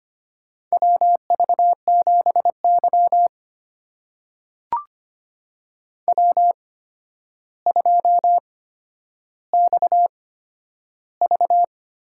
Morse code, heard as a keyed tone, about 25 wpm, spaced as for 11 wpm, 700 Hz. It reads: WV7Y W2XV
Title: USA Call Signs - 4 Characters - 8x Character Spacing - 25wpm